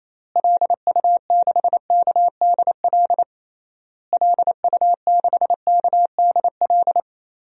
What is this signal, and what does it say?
Morse code, heard as a keyed tone, about 28 wpm, 700 Hz. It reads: LU6KDL LU6KDL
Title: DXCC-100 Call Signs - 6 Characters - Repeated 3x - 28wpm